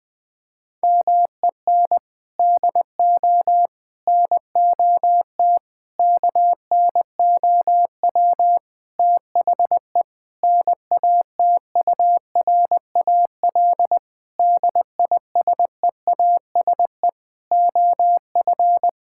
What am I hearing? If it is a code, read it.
Morse code, heard as a keyed tone, about 20 wpm, 700 Hz. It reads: MEN DO NOT KNOW THE NATURAL DISEASE OF
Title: Sentences from Top 1000 Words - Review Part 6 - 20wpm